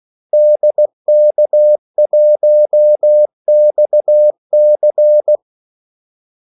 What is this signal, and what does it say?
Morse code, heard as a keyed tone, about 16 wpm, 600 Hz. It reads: DK1XC
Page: Kaldesignaler 16 wpm